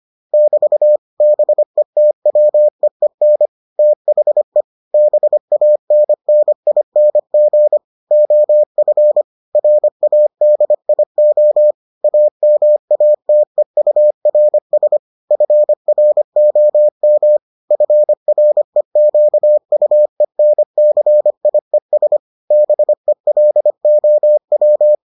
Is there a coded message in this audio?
Yes: Morse code from 0.3 to 24.9 s, heard as a keyed tone, about 25 wpm, 600 Hz. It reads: <BT> BETWEEN THE BANNING OF RADIO AMATEURS FROM FREQUENCIES BELOW